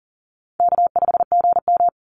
Morse code, heard as a keyed tone, about 40 wpm, 700 Hz. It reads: K5GM
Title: Call Signs Repeated 3x - 40wpm